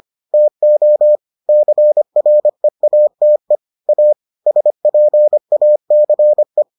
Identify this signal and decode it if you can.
Morse code, heard as a keyed tone, about 25 wpm, 600 Hz. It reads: TO CREATE A SPACE